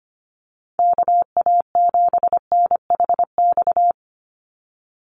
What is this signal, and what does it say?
Morse code, heard as a keyed tone, about 25 wpm, 700 Hz. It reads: KA7NHX
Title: USA Call Signs 25wpm